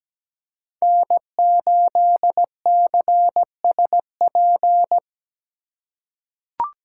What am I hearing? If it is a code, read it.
Morse code, heard as a keyed tone, about 17 wpm, 700 Hz. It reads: N8CSP
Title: USA Call Signs - 5 Characters - 8x Character Spacing - 17wpm